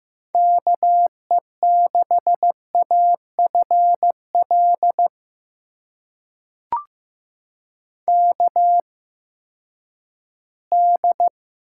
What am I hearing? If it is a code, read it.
Morse code, heard as a keyed tone, about 15 wpm, spaced as for 10 wpm, 700 Hz. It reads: KE6AFL KD
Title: USA Call Signs - 6 Characters - 8x Character Spacing - 15wpm